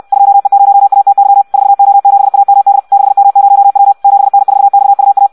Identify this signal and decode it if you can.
Morse code, heard as a keyed tone, about 64 wpm, 800 Hz. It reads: 2E0ATY 4O7AMD 6M0W 7N4XUA
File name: Audio practica CW / qrq-tx-66 2